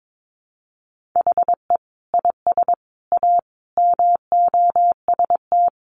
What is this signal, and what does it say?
Morse code, heard as a keyed tone, about 22 wpm, 700 Hz. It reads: HE IS A MOST
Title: Sentences from Top 1000 Words 22wpm